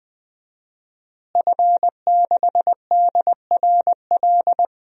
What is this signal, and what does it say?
Morse code, heard as a keyed tone, about 20 wpm, 700 Hz. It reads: F6DRL
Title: DXCC-100 Call Signs - 5 Characters - Repeated 3x - 20wpm